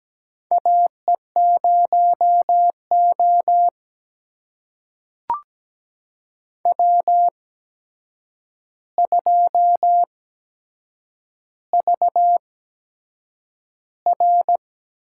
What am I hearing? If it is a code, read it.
Morse code, heard as a keyed tone, about 17 wpm, spaced as for 7 wpm, 700 Hz. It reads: AE0O W2VR
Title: USA Call Signs - 4 Characters - 8x Character Spacing - 17wpm